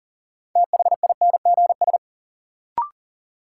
Morse code, heard as a keyed tone, about 40 wpm, 700 Hz. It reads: THINGS
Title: Sentences from Top 1400 Words - New Word Drill Alt - 40wpm